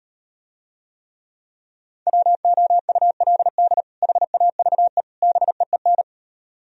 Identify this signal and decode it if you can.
Morse code, heard as a keyed tone, about 38 wpm, 700 Hz. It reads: WOULD HAVE BEEN